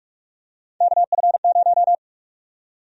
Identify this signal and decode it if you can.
Morse code, heard as a keyed tone, about 45 wpm, 700 Hz. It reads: KF0
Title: USA Call Sign Prefixes - 45wpm